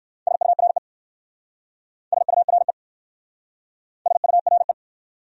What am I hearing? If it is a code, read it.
Morse code, heard as a keyed tone, about 59 wpm, 700 Hz. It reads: SURE SURE SURE